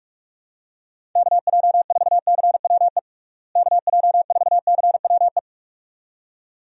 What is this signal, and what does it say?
Morse code, heard as a keyed tone, about 45 wpm, 700 Hz. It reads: KJ4CWE KJ4CWE
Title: USA Call Signs - 6 Characters - Repeated 2x - 45wpm